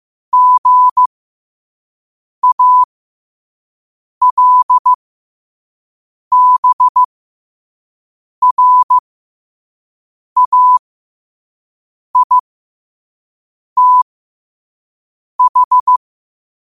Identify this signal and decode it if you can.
Morse code, heard as a keyed tone, about 15 wpm, spaced as for 5 wpm, 1000 Hz. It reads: GALBRAITH
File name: 2026-03-16-5wpm